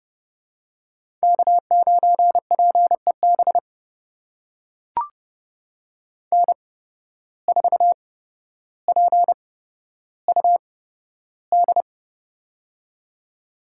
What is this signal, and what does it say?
Morse code, heard as a keyed tone, about 30 wpm, spaced as for 13 wpm, 700 Hz. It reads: K9PEB N4PUD
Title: USA Call Signs - 8x Character Spacing - 30wpm